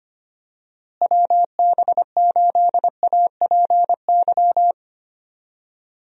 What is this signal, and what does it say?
Morse code, heard as a keyed tone, about 25 wpm, 700 Hz. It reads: WB8APY
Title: USA Call Signs - 6 Characters - Repeated 2x - 25wpm